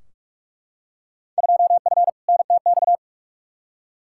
Morse code, heard as a keyed tone, about 43 wpm, 700 Hz. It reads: 2F NTX
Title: Field Day Exchanges - 45wpm